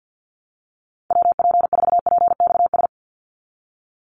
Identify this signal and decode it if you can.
Morse code, heard as a keyed tone, about 50 wpm, 700 Hz. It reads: WP4PXS
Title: USA Call Signs 50wpm